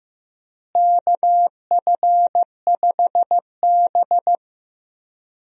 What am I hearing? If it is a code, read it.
Morse code, heard as a keyed tone, about 15 wpm, 700 Hz. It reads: KF5B
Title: USA Call Signs - 4 Characters - Repeated 2x - 15wpm